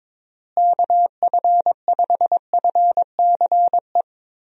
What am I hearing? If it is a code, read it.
Morse code, heard as a keyed tone, about 22 wpm, 700 Hz. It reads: KF5FCE
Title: USA Call Signs - 6 Characters - Repeated 2x - 22wpm